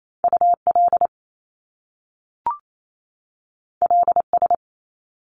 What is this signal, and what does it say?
Morse code, heard as a keyed tone, about 28 wpm, 700 Hz. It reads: UL LS
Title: Top Two Letter Combinations in Words 28wpm